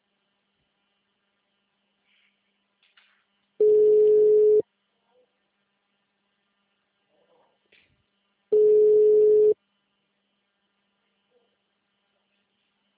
Ringback tone (Europe, 425 Hz).
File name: out- -2004- - - .19403